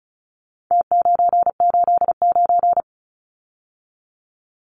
Morse code, heard as a keyed tone, about 34 wpm, 700 Hz. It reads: T989
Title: Numbers - Contest Style - 4-Digit 35wpm